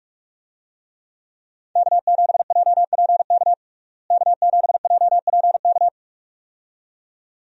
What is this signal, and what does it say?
Morse code, heard as a keyed tone, about 43 wpm, 700 Hz. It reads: K7JPK K7JPK
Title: USA Call Signs - 5 Characters - Repeated 2x - 45wpm